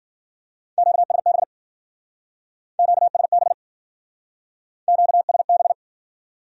Morse code, heard as a keyed tone, about 52 wpm, 700 Hz. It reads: QSB QSB QSB